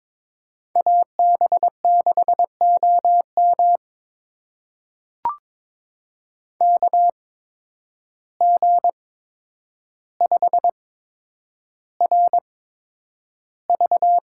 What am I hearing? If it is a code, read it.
Morse code, heard as a keyed tone, about 22 wpm, spaced as for 9 wpm, 700 Hz. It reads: AB6OM KG5RV
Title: USA Call Signs - 8x Character Spacing - 22wpm